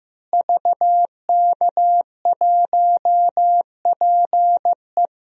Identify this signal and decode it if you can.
Morse code, heard as a keyed tone, about 16 wpm, 700 Hz. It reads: VK1PE
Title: DXCC-100 Call Signs - 5 Characters - Repeated 3x - 15wpm